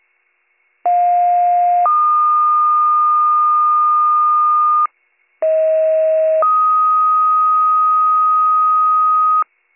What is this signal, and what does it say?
Two-tone page, heard as a one-second tone, then a three-second tone, about 710 then 1220 Hz, about 640 then 1250 Hz. It is heard from 0.8 to 9.4 s.